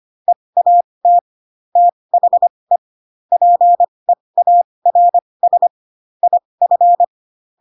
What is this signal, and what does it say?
Morse code, heard as a keyed tone, about 25 wpm, spaced as for 20 wpm, 700 Hz. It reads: EAT THE PEARS IF
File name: Pinocchio / Pinocchio ch7 15 25 ik1hge